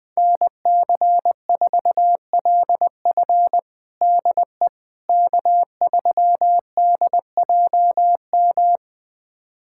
Morse code, heard as a keyed tone, about 20 wpm, 700 Hz. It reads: NC4LF DE K3DJM
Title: USA Callsign DE Callsign - 5 Characters - 20wpm